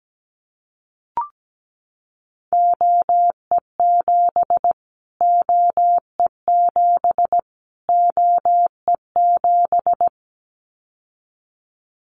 Morse code, heard as a keyed tone, about 17 wpm, 700 Hz. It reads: OE7 OE7 OE7
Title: Callsign Prefixes for Top 25 DXCC - Repeated 3x - 17wpm